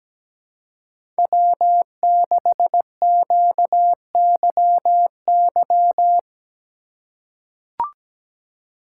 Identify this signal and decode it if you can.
Morse code, heard as a keyed tone, about 17 wpm, 700 Hz. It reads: W6QYY
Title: USA Call Signs - 4x Character Spacing - 17wpm